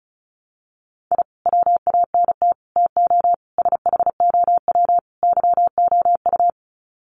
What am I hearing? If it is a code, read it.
Morse code, heard as a keyed tone, about 34 wpm, 700 Hz. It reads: I WANT TO SHOW YOU
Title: Sets of 5 Words 35wpm